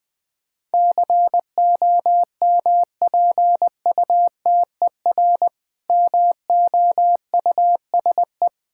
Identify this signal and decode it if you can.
Morse code, heard as a keyed tone, about 20 wpm, 690 Hz. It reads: COMPUTER MOUSE
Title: Difficult Words - Speed-Racing with Hints - 20wpm